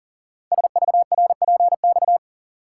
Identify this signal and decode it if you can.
Morse code, heard as a keyed tone, about 40 wpm, 700 Hz. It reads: SVRPX
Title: Code Groups - 5 Letters - 40wpm